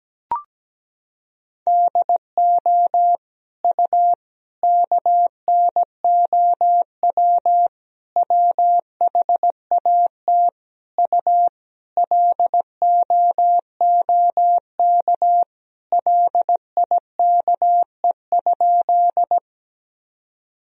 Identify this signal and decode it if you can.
Morse code, heard as a keyed tone, about 17 wpm, 700 Hz. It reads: DO U KNOW WHAT U LOOK LIKE?
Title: Sentences from Top 100 Words and Abbreviations 17wpm